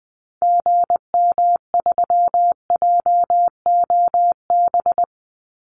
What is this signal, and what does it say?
Morse code, heard as a keyed tone, about 20 wpm, 700 Hz. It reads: GM3JOB
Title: Call Signs Repeated 2x - 20wpm